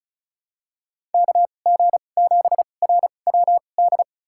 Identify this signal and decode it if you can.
Morse code, heard as a keyed tone, about 34 wpm, spaced as for 25 wpm, 700 Hz. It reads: KG7RWD
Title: USA Call Signs - 6 Characters - 2x Character Spacing - 35wpm